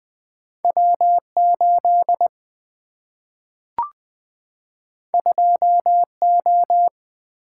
Morse code, heard as a keyed tone, about 20 wpm, 700 Hz. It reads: W8 2O
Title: Single-Letter Single-Number 20wpm